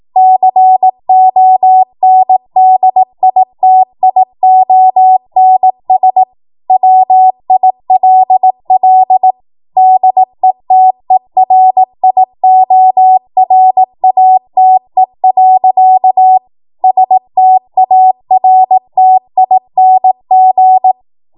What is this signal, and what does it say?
Morse code, heard as a keyed tone, about 18 wpm, 750 Hz. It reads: CONDITIONS WILL DETERIORATE. STARTING
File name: Morse / 18WPM